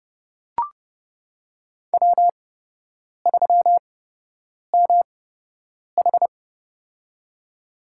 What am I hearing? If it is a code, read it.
Morse code, heard as a keyed tone, about 30 wpm, spaced as for 8 wpm, 700 Hz. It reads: W3MH
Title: USA Call Signs - 4 Characters - 8x Character Spacing - 30wpm